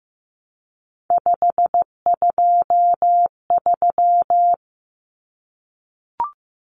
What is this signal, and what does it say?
Morse code, heard as a keyed tone, about 15 wpm, 700 Hz. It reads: 523
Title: Numbers - 3-Digit 15wpm